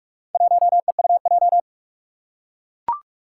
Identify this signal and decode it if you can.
Morse code, heard as a keyed tone, about 44 wpm, 700 Hz. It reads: 1EUJ
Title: Four Challenging Letter-Number Combinations 45wpm